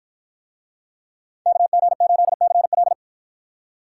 Morse code, heard as a keyed tone, about 49 wpm, 700 Hz. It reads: KG8CL